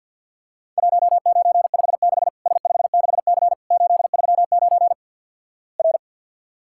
Morse code, heard as a keyed tone, about 50 wpm, 700 Hz. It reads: 1956 S56C 839 R
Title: Worked All Europe DX Contest - QTC Exchange with Repeats - 50wpm